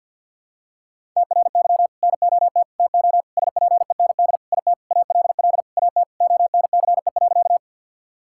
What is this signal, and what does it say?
Morse code, heard as a keyed tone, about 47 wpm, 700 Hz. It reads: TRY NOT TO SPEND IT ALL AT ONCE.
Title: Sentences from Top 1000 Words - Review Part 5 - 50wpm